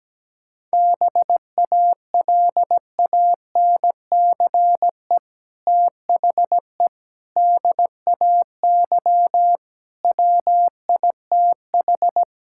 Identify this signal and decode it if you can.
Morse code, heard as a keyed tone, about 17 wpm, 700 Hz. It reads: BALANCE THE DAY WITH
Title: Sentences from Top 1000 Words - Review Part 6 - 17wpm